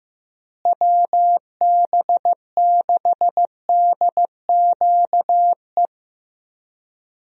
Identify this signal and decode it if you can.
Morse code, heard as a keyed tone, about 15 wpm, 700 Hz. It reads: WB6DQE
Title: USA Call Signs - 6 Characters - Repeated 2x - 15wpm